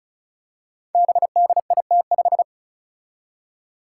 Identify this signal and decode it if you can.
Morse code, heard as a keyed tone, about 36 wpm, 700 Hz. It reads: BDIT5
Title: Five Challenging Letter-Number Combinations 35wpm